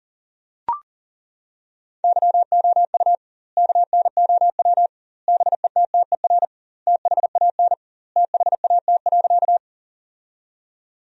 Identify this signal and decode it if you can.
Morse code, heard as a keyed tone, about 40 wpm, 700 Hz. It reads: YOU KNOW BETTER THAN THAT.
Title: 2x Word-Spacing - Sentences from Top 400 Words - 40wpm